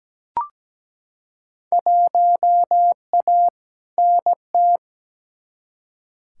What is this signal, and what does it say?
Morse code, heard as a keyed tone, about 17 wpm, 700 Hz. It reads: 1A NT
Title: Field Day Exchanges Encore - 17wpm